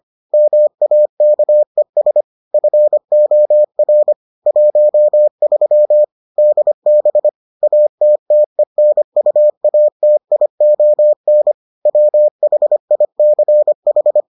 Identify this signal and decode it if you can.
Morse code, heard as a keyed tone, about 25 wpm, 600 Hz. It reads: MAKES FOR 13 DB ATTENUATION WHICH